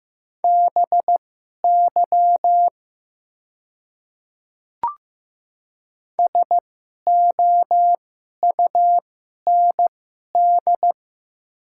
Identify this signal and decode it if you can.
Morse code, heard as a keyed tone, about 15 wpm, spaced as for 11 wpm, 700 Hz. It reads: BY SOUND
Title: 2x Character Spacing - Top 300 Words - 15wpm